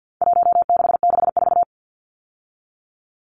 Morse code, heard as a keyed tone, about 50 wpm, 700 Hz. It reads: 1664
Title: Numbers - Contest Style - 4-Digit 50wpm